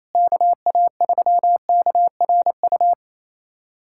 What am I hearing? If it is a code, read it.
Morse code, heard as a keyed tone, about 28 wpm, 700 Hz. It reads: KA3KRU
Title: USA Callsign DE Callsign 28wpm